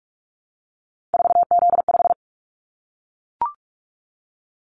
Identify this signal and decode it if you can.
Morse code, heard as a keyed tone, about 44 wpm, 700 Hz. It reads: 4Z5
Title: Callsign Prefixes for Top 50 DXCC - 45wpm